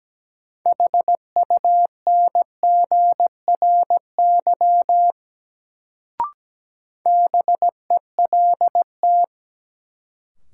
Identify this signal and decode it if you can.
Morse code, heard as a keyed tone, about 17 wpm, 700 Hz. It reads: HUNGRY BELT
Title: Sentences from Top 1400 Words - New Word Drill - 17wpm